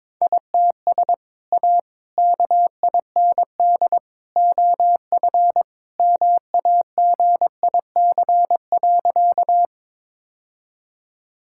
Morse code, heard as a keyed tone, about 22 wpm, 700 Hz. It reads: ITS A KIND OF MAGIC.
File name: practice / 22wpm / Sentences with 3-Word Phrases - 22wpm